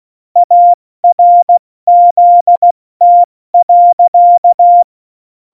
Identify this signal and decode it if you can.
Morse code, heard as a keyed tone, about 16 wpm, 700 Hz. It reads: ARZT.
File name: Morsekurs / Witze / Witz 22. September 2020, 70 F80